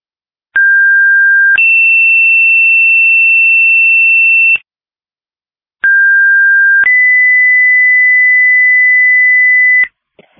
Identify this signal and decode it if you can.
Two-tone page, heard as a one-second tone, then a three-second tone, about 1600 then 2700 Hz, about 1600 then 2000 Hz, with noise. It is heard from 0.6 to 9.9 s.